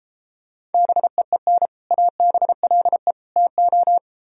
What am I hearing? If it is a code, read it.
Morse code, heard as a keyed tone, about 32 wpm, 700 Hz. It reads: BEEN ABLE TO